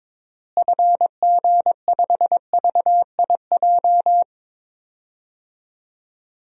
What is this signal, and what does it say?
Morse code, heard as a keyed tone, about 22 wpm, 700 Hz. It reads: FG5VIJ
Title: DXCC-100 Call Signs - 6 Characters - Repeated 2x - 22wpm